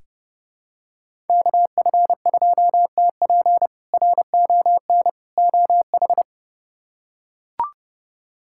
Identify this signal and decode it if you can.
Morse code, heard as a keyed tone, about 31 wpm, 700 Hz. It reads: KF2TP RON OH
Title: K1USN SST QSO Practice - 30wpm